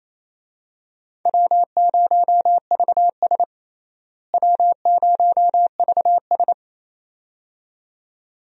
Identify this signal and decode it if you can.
Morse code, heard as a keyed tone, about 28 wpm, 700 Hz. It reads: W0VS W0VS
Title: USA Call Signs - 4 Characters - Repeated 2x - 28wpm